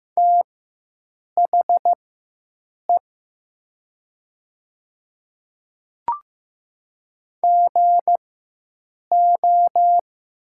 Morse code, heard as a keyed tone, about 15 wpm, spaced as for 7 wpm, 700 Hz. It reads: THE GO
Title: 4x Character Spacing - Top 100 Words - 15wpm